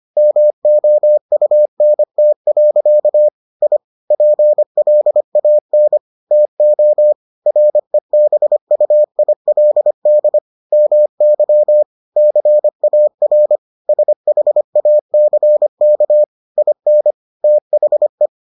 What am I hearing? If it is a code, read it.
Morse code, heard as a keyed tone, about 25 wpm, 600 Hz. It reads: MOUNT. I PLAN TO REBUILD MY CAR SHACK IN THE